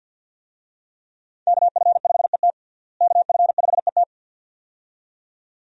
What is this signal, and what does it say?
Morse code, heard as a keyed tone, about 48 wpm, 700 Hz. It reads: KF5ET KF5ET
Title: USA Call Signs - 5 Characters - Repeated 2x - 50wpm